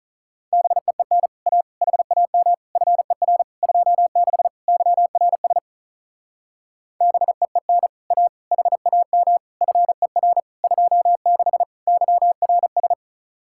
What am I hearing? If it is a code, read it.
Morse code, heard as a keyed tone, about 42 wpm, 700 Hz. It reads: BEEN A HAM FER 26 YRS BEEN A HAM FER 26 YRS
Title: Speed-Racing Common QSO Phrases - 35wpm